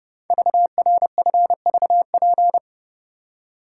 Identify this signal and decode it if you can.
Morse code, heard as a keyed tone, about 30 wpm, 700 Hz. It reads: VRFVP
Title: Code Groups - 5 Letters - 30wpm